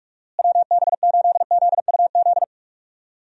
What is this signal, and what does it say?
Morse code, heard as a keyed tone, about 46 wpm, 700 Hz. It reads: WB8ZUZ
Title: USA Call Signs - 6 Characters - Repeated 2x - 45wpm